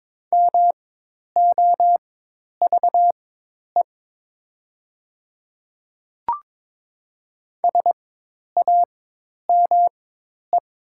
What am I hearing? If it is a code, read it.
Morse code, heard as a keyed tone, about 22 wpm, spaced as for 10 wpm, 700 Hz. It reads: MOVE SAME
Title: 4x Character Spacing - Top 200 Words - 22wpm